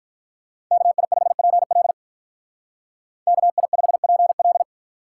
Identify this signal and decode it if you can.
Morse code, heard as a keyed tone, about 44 wpm, 700 Hz. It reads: KI5PL KI5PL